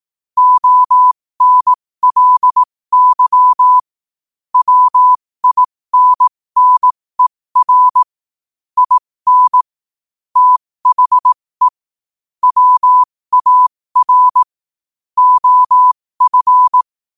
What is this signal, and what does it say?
Morse code, heard as a keyed tone, about 18 wpm, spaced as for 15 wpm, 1000 Hz. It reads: ONLY WINNER IN THE WAR OF
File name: qotdfiles / 15wpm